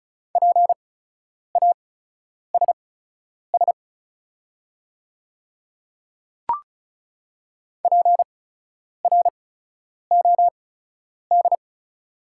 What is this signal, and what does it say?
Morse code, heard as a keyed tone, about 36 wpm, spaced as for 10 wpm, 700 Hz. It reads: PASS PROD
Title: 8x Character Spacing - Top 500 Words - 35wpm